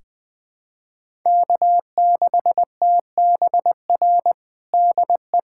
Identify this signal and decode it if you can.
Morse code, heard as a keyed tone, about 20 wpm, 700 Hz. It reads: K6TBR DE
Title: USA Callsign DE Callsign - 5 Characters - 20wpm